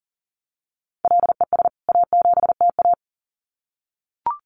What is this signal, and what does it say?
Morse code, heard as a keyed tone, about 40 wpm, 700 Hz. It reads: LES A7TA
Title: CWT Exchange 40wpm